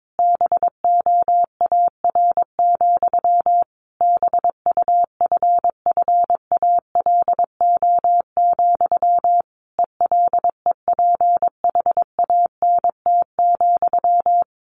Morse code, heard as a keyed tone, about 22 wpm, 700 Hz. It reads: BOAR, BUFFALO, ELEPHANT,
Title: A Sound Of Thunder 22wpm